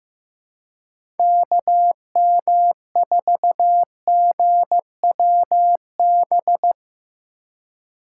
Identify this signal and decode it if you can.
Morse code, heard as a keyed tone, about 15 wpm, 700 Hz. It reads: KM4GWB
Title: USA Call Signs - 2x Character Spacing - 15wpm